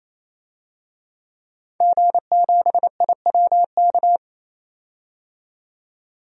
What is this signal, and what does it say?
Morse code, heard as a keyed tone, about 28 wpm, 700 Hz. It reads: G7IWK
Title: DXCC-100 Call Signs - 5 Characters - 28wpm